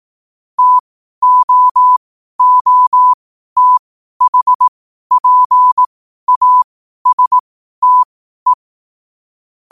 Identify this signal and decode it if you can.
Morse code, heard as a keyed tone, about 18 wpm, spaced as for 12 wpm, 1000 Hz. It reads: TOOTHPASTE
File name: qotdfiles / 12wpm